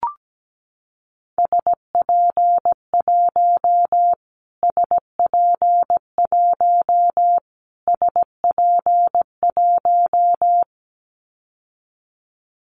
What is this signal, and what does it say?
Morse code, heard as a keyed tone, about 17 wpm, 700 Hz. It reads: SP1 SP1 SP1
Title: Callsign Prefixes for Top 25 DXCC - Repeated 3x - 17wpm